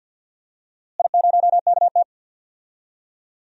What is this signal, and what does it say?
Morse code, heard as a keyed tone, about 46 wpm, 700 Hz. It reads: I0KT
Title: DXCC-100 Call Signs - 4 Characters - Repeated 3x - 50wpm